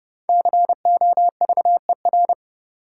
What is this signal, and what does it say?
Morse code, heard as a keyed tone, about 31 wpm, 700 Hz. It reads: COVER
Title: Five Letter Words - 30wpm